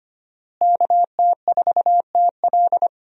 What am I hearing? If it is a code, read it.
Morse code, heard as a keyed tone, about 25 wpm, 700 Hz. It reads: KT4TL
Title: USA Call Signs - 5 Characters - Repeated 2x - 25wpm